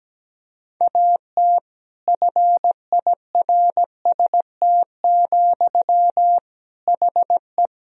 Morse code, heard as a keyed tone, about 17 wpm, 700 Hz. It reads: AT FIRST, HE
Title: Sentences from Top 1000 Words - Review Part 4 - 17wpm